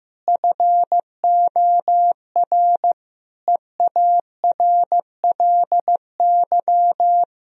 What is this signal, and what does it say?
Morse code, heard as a keyed tone, about 15 wpm, 700 Hz. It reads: FOR EARLY
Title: Sentences from Top 1000 Words - Review Part 4 - 15wpm